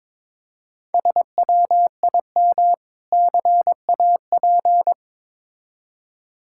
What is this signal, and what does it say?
Morse code, heard as a keyed tone, about 22 wpm, 700 Hz. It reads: SWIM CAP
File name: Difficult Words with Hints - 22wpm